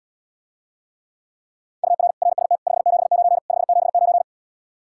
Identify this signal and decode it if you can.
Morse code, heard as a keyed tone, about 77 wpm, 700 Hz. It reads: UR RST 579 579